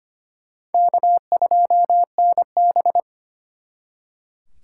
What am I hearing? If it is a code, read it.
Morse code, heard as a keyed tone, about 33 wpm, 700 Hz. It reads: K2NB